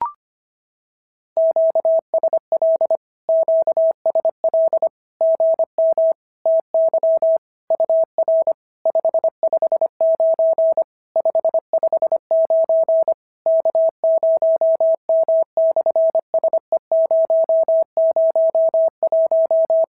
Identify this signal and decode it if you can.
Morse code, heard as a keyed tone, about 25 wpm, 650 Hz. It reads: QSL QSL GM TY UR 559 559 K0M/SE001
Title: SOTA QSO Practice 25wpm